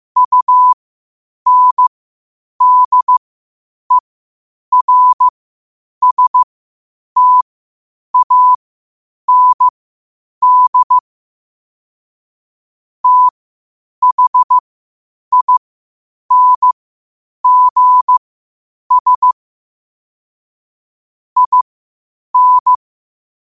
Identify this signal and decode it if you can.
Morse code, heard as a keyed tone, about 15 wpm, spaced as for 8 wpm, 1000 Hz. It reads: UNDERSTAND THINGS IN
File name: qotdfiles / 8wpm